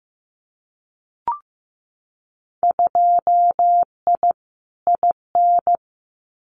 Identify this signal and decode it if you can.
Morse code, heard as a keyed tone, about 15 wpm, 700 Hz. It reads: 2I IN
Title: Winter Field Day Exchanges Encore 15wpm